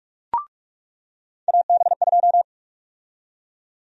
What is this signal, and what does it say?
Morse code, heard as a keyed tone, about 41 wpm, 700 Hz. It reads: AB2
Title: USA Call Sign Prefixes - 45wpm